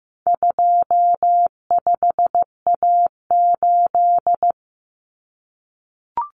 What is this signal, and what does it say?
Morse code, heard as a keyed tone, about 15 wpm, 700 Hz. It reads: 25A8
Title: Common Cut Numbers - 4-Digit 15wpm